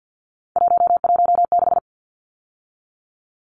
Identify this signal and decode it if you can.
Morse code, heard as a keyed tone, about 50 wpm, 700 Hz. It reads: 116
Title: Numbers - 3-Digit 50wpm